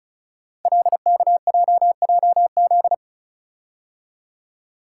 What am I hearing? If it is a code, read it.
Morse code, heard as a keyed tone, about 36 wpm, 700 Hz. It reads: LKJJZ
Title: Code Groups - 5 Letters - 35wpm